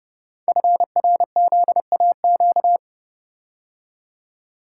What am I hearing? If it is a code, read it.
Morse code, heard as a keyed tone, about 30 wpm, 700 Hz. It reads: FRZAQ
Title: Code Groups - 5 Letters - 30wpm